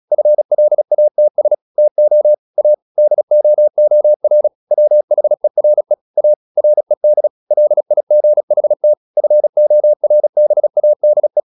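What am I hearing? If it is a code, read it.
Morse code, heard as a keyed tone, about 36 wpm, 600 Hz. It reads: FLATS TO A DOOR WHERE A RED LIGHT FORBADE
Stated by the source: Morse practice at 36 wpm, text of Ken Follett's novel Never, from CW med Gnister.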